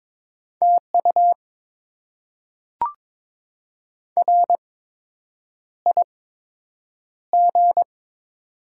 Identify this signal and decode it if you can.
Morse code, heard as a keyed tone, about 22 wpm, spaced as for 8 wpm, 700 Hz. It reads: TU RIG
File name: Top 20 QSO Elements - 8x Character Spacing - 22wpm